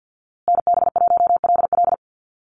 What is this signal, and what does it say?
Morse code, heard as a keyed tone, about 50 wpm, 700 Hz. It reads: NB1LL
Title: USA Call Signs Repeated 3x 50wpm